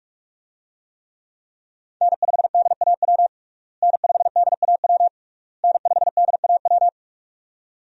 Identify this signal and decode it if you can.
Morse code, heard as a keyed tone, about 46 wpm, 700 Hz. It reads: N5DAW N5DAW N5DAW
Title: USA Call Signs - 5 Characters - Repeated 3x - 45wpm